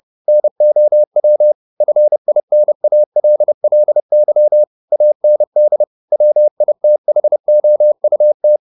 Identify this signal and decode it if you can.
Morse code, heard as a keyed tone, about 30 wpm, 600 Hz. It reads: NOW FINALLY AND WITHOUT